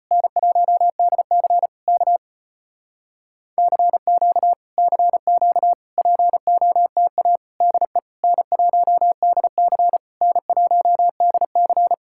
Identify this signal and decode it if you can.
Morse code, heard as a keyed tone, about 34 wpm, 700 Hz. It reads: N1DC K CQ CQ POTA DE N1DC N1DC